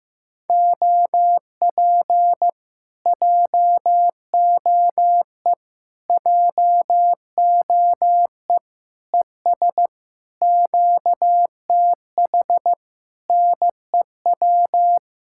Morse code, heard as a keyed tone, about 15 wpm, 700 Hz. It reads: OP JOE JOE ES QTH NEW
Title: Speed-Racing Common QSO Phrases Encore - 15wpm